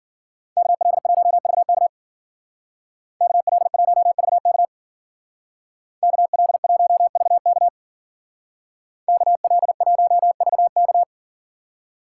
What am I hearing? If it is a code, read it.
Morse code, heard as a keyed tone, about 53 wpm, 700 Hz. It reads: KL1VK KL1VK KL1VK KL1VK